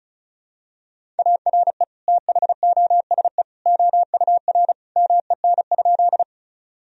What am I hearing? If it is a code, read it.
Morse code, heard as a keyed tone, about 36 wpm, 700 Hz. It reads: ARE THOSE OUR MEN?
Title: Sentences from Top 1000 Words - Review Part 1 - 35wpm